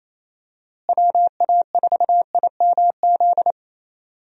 Morse code, heard as a keyed tone, about 28 wpm, 700 Hz. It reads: WA4IMZ
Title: USA Call Signs - 6 Characters - Repeated 3x - 28wpm